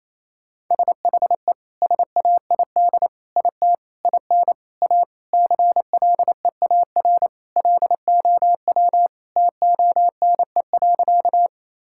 Morse code, heard as a keyed tone, about 27 wpm, 700 Hz. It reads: SHE SAID IT IN A CLEAR LOW TONE.
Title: Sentences from Top 1600 Words 28wpm